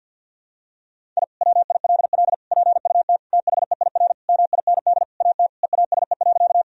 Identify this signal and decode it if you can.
Morse code, heard as a keyed tone, about 51 wpm, 700 Hz. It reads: I WILL PUT THEIR MIND AT EASE.
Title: Sentences from Top 1000 Words - Review Part 1 - 50wpm